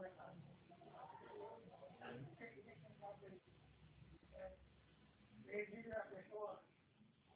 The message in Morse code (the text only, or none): none